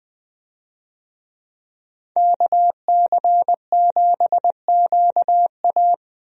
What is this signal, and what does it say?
Morse code, heard as a keyed tone, about 20 wpm, 700 Hz. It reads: KC7QA
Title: USA Call Signs - 5 Characters - 20wpm